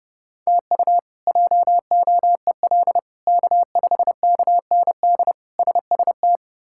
Morse code, heard as a keyed tone, about 32 wpm, 700 Hz. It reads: TU JOEL K5KND SST
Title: K1USN SST QSO Practice - 30wpm